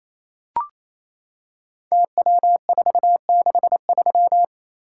Morse code, heard as a keyed tone, about 29 wpm, 700 Hz. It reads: TW463
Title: Code Groups - 5 Letter:Numbers - 28wpm